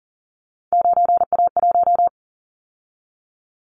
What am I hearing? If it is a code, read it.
Morse code, heard as a keyed tone, about 40 wpm, 700 Hz. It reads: 9A1
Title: Callsign Prefixes for Top 100 DXCC - 40wpm